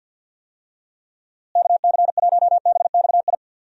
Morse code, heard as a keyed tone, about 46 wpm, 700 Hz. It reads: KC1BXI
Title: USA Call Signs - 6 Characters - Repeated 2x - 50wpm